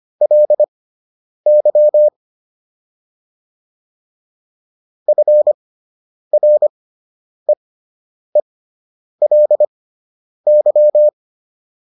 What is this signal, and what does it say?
Morse code, heard as a keyed tone, about 25 wpm, spaced as for 9 wpm, 600 Hz. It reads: LY FREELY